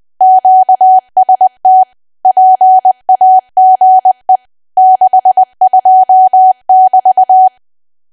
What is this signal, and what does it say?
Morse code, heard as a keyed tone, about 20 wpm, 750 Hz. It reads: QST PAGE 62<BT>
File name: cwa / W1AW-Level3 / 20WPM